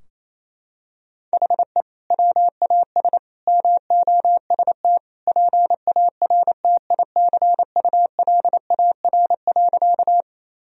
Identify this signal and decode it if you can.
Morse code, heard as a keyed tone, about 28 wpm, 700 Hz. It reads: HE WAS MOST PARTICULAR.
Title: Sentences from Top 1000 Words - Review Part 4 - 28wpm